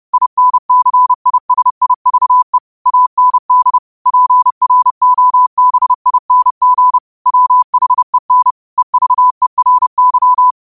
Morse code, heard as a keyed tone, about 30 wpm, 1000 Hz. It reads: INCISIVE AND PROBING WHEN EVERY
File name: 30wpm